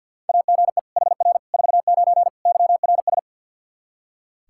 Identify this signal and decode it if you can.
Morse code, heard as a keyed tone, about 45 wpm, 700 Hz. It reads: AGE HR 49 YRS
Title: K1USN SST QSO Practice 50wpm